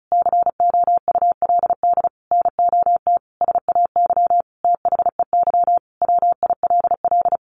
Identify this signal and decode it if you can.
Morse code, heard as a keyed tone, about 34 wpm, 700 Hz. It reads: COULD NOT SAY THEY WILL
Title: Sentences from Top 1100 Words 35wpm